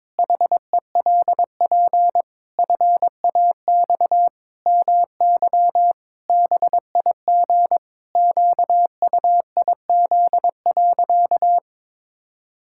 Morse code, heard as a keyed tone, about 22 wpm, 700 Hz. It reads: HELP FAX MY BIG QUIZ.
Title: Pangram Challenge v2 - 22wpm